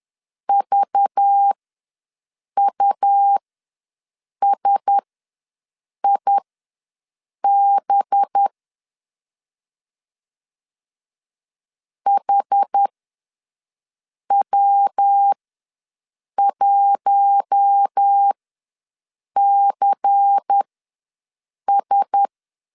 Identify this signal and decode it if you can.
Morse code, heard as a keyed tone, about 10 wpm, spaced as for 6 wpm, 800 Hz. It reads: VUSIB HW1CS
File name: LEZIONI / 5 LBVC / 5 LBVC ws 10wpm 3733